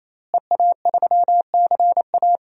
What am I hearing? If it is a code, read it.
Morse code, heard as a keyed tone, about 28 wpm, 700 Hz. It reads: EA3CA
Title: DXCC-100 Call Signs - 5 Characters - Repeated 3x - 28wpm